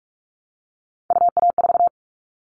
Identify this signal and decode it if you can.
Morse code, heard as a keyed tone, about 46 wpm, 690 Hz. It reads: UA4
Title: Callsign Prefixes for Top 50 DXCC - 45wpm